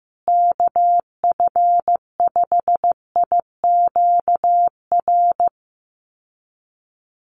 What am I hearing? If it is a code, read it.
Morse code, heard as a keyed tone, about 15 wpm, 700 Hz. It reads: KF5IQR
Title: USA Call Signs Repeated 3x 15wpm